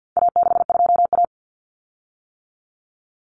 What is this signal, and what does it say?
Morse code, heard as a keyed tone, about 50 wpm, 700 Hz. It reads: A62A
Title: Common Cut Numbers - 4-Digit 50wpm